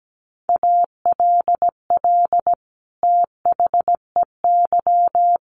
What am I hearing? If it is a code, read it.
Morse code, heard as a keyed tone, about 17 wpm, 700 Hz. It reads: ALL THEY
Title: Sentences from Top 1300 Words 17wpm